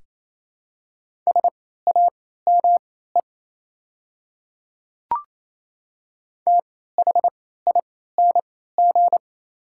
Morse code, heard as a keyed tone, about 28 wpm, spaced as for 16 wpm, 700 Hz. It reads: SAME THING
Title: 3x Character Spacing - Top 200 Words - 28wpm